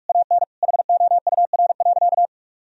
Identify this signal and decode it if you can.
Morse code, heard as a keyed tone, about 42 wpm, 700 Hz. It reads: AN HOUR.
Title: Sentences from Top 1000 Words - Review Part 1 - 45wpm